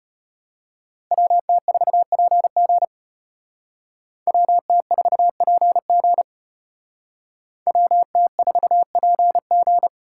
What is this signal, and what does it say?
Morse code, heard as a keyed tone, about 32 wpm, 700 Hz. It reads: WT4PG WT4PG WT4PG